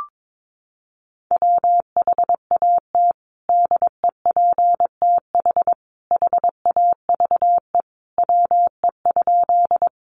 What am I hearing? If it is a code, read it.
Morse code, heard as a keyed tone, about 22 wpm, 700 Hz. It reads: WHAT DEPTH HAVE WE?
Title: Sentences from Top 1200 Words 22wpm